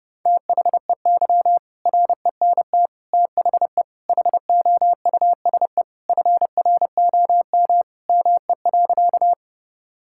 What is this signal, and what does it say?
Morse code, heard as a keyed tone, about 30 wpm, 700 Hz. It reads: THEY RENT THE HOUSE FROM ME.
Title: Sentences from Top 1000 Words - Review Part 5 - 30wpm